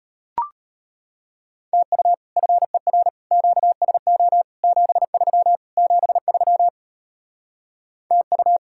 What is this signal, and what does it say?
Morse code, heard as a keyed tone, about 33 wpm, 700 Hz. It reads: TU FER QSO 73 73 TU